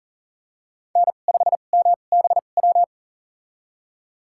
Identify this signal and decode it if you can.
Morse code, heard as a keyed tone, about 40 wpm, spaced as for 29 wpm, 700 Hz. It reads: N5MBW
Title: USA Call Signs - 5 Characters - 2x Character Spacing - 40wpm